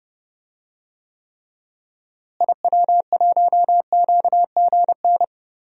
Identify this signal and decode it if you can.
Morse code, heard as a keyed tone, about 30 wpm, 700 Hz. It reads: IW1QGN
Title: DXCC-100 Call Signs - 6 Characters - Repeated 2x - 30wpm